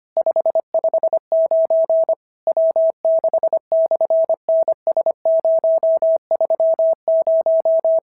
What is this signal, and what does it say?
Morse code, heard as a keyed tone, about 24 wpm, 650 Hz. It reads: 559 W6/NS030
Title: SOTA QSO Practice 25wpm